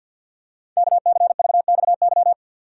Morse code, heard as a keyed tone, about 49 wpm, 700 Hz. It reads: KCVXY
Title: Code Groups - 5 Letters - 50wpm